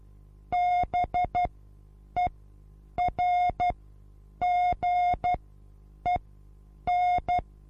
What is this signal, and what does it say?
Morse code, heard as a keyed tone, about 12 wpm, spaced as for 8 wpm, 720 Hz. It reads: BERGEN
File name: repos / Morsekurs / DJ4UF Morsen23